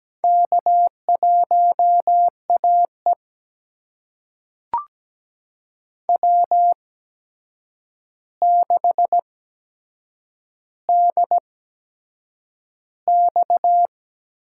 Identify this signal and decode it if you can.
Morse code, heard as a keyed tone, about 17 wpm, spaced as for 7 wpm, 700 Hz. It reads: K1AE W6DX
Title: USA Call Signs - 4 Characters - 8x Character Spacing - 17wpm